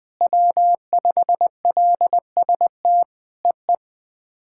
Morse code, heard as a keyed tone, about 20 wpm, 700 Hz. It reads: W5LST EE
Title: POTA QSO Practice - 20wpm